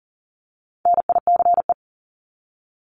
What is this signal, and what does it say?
Morse code, heard as a keyed tone, about 40 wpm, 700 Hz. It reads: NICE
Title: Top 751-1000 Words 40wpm